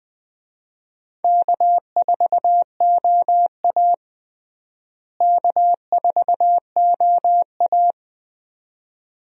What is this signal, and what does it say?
Morse code, heard as a keyed tone, about 20 wpm, 700 Hz. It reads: K4OA K4OA
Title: USA Call Signs - 4 Characters - Repeated 2x - 20wpm